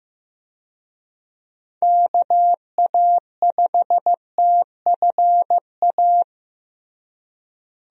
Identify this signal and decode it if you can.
Morse code, heard as a keyed tone, about 16 wpm, 700 Hz. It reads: KA5TFA
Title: USA Call Signs - 6 Characters - Repeated 2x - 15wpm